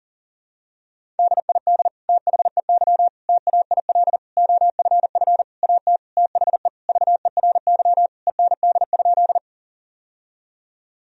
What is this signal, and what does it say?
Morse code, heard as a keyed tone, about 40 wpm, 700 Hz. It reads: DID THEY TAIL OFF AT THE VERY END?
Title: Sentences from Top 1000 Words - Review Part 3 - 40wpm